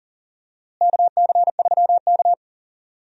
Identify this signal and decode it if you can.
Morse code, heard as a keyed tone, about 40 wpm, 700 Hz. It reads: KC3K
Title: USA Call Signs - 4 Characters - Repeated 2x - 40wpm